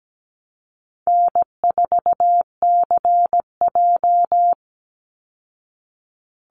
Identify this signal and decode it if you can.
Morse code, heard as a keyed tone, about 17 wpm, 700 Hz. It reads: N4CJ
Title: Easy Call Signs Repeated 3x - 17wpm